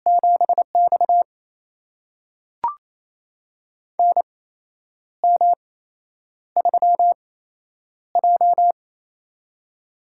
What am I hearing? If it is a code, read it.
Morse code, heard as a keyed tone, about 27 wpm, spaced as for 9 wpm, 700 Hz. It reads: 7X NM3J
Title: USA Call Signs - 4 Characters - 8x Character Spacing - 28wpm